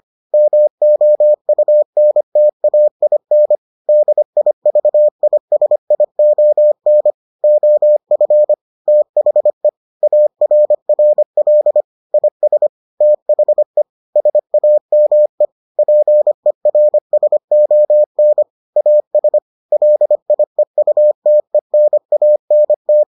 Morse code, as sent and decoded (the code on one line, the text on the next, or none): -- --- ..- -. - .- .. -. / -.. .. ...- .. ... .. --- -. / --- ..-. / - .... / .- .-. .-. .-.. / .. ... / - .... / ... .- -- . / .--. .-. ... --- -. / .- ... / .-.. .. ..- - . -. .- -. -
MOUNTAIN DIVISION OF THE ARRL IS THE SAME PERSON AS LIEUTENANT